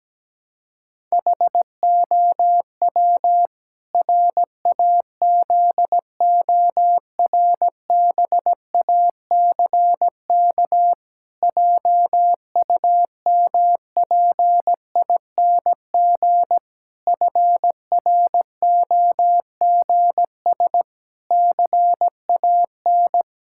Morse code, heard as a keyed tone, about 17 wpm, 700 Hz. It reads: HOW RAZORBACK JUMPING FROGS CAN
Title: Pangram Challenge v2 - 17wpm